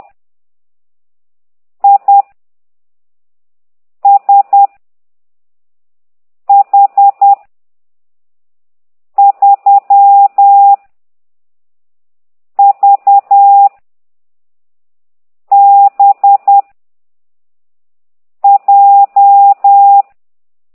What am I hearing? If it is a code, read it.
Morse code, heard as a keyed tone, about 10 wpm, 800 Hz. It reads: I S H 3 V B J